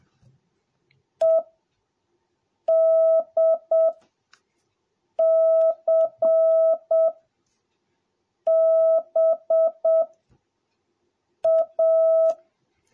Morse code, heard as a keyed tone, about 7 wpm, spaced as for 4 wpm, 650 Hz. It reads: EDCBA